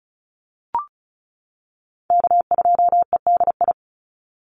Morse code, heard as a keyed tone, about 34 wpm, 700 Hz. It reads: K2EDI
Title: USA Call Signs 35wpm